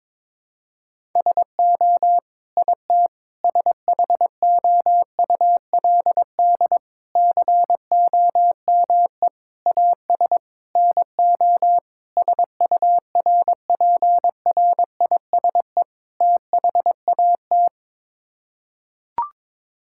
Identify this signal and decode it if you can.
Morse code, heard as a keyed tone, about 22 wpm, 700 Hz. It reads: SO IT SHOULD COME AS NO SURPRISE THAT
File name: practice / 22wpm / Sets of 8 Words - 22wpm